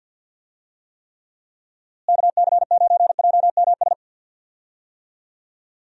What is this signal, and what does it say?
Morse code, heard as a keyed tone, about 51 wpm, 700 Hz. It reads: KC9JMS
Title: USA Call Signs - 6 Characters - Repeated 2x - 50wpm